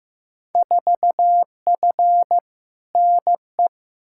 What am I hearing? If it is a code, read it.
Morse code, heard as a keyed tone, about 15 wpm, 700 Hz. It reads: 4F NE
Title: Field Day Exchanges Encore - 15wpm